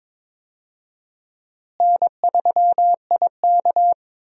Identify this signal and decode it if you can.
Morse code, heard as a keyed tone, about 22 wpm, 700 Hz. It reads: N3IK
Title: USA Call Signs - 4 Characters - Repeated 3x - 22wpm